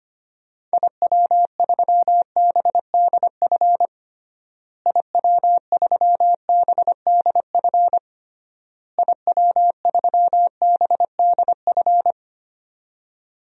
Morse code, heard as a keyed tone, about 25 wpm, 700 Hz. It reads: IW3BDF IW3BDF IW3BDF
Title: DXCC-100 Call Signs - 6 Characters - Repeated 3x - 25wpm